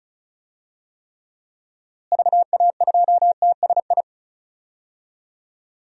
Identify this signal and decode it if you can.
Morse code, heard as a keyed tone, about 36 wpm, 700 Hz. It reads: VA2TSI
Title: DXCC-100 Call Signs - 6 Characters - Repeated 2x - 35wpm